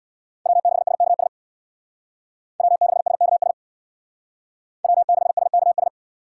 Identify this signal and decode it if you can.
Morse code, heard as a keyed tone, about 64 wpm, 700 Hz. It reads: W6SKS W6SKS W6SKS